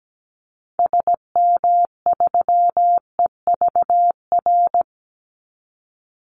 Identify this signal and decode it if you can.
Morse code, heard as a keyed tone, about 17 wpm, 700 Hz. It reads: SM3EVR
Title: Call Signs - 17wpm